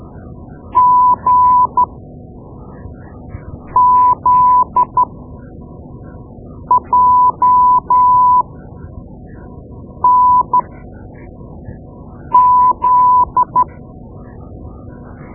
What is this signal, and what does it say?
Morse code, heard as a keyed tone, about 13 wpm, spaced as for 4 wpm, 1000 Hz. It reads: GZJNZ
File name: Morse Lessons / Morse Lessons / Tape-2B